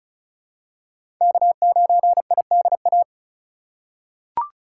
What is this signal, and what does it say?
Morse code, heard as a keyed tone, about 36 wpm, 690 Hz. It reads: K9IDA
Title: USA Call Signs - 5 Characters - 3x Character Spacing - 35wpm